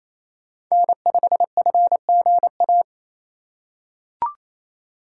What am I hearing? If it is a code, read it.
Morse code, heard as a keyed tone, about 28 wpm, 700 Hz. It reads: N5FGA
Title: USA Call Signs - 5 Characters - 4x Character Spacing - 28wpm